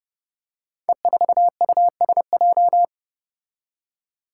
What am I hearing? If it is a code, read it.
Morse code, heard as a keyed tone, about 29 wpm, 700 Hz. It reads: E4USJ
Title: Five Challenging Letter-Number Combinations 30wpm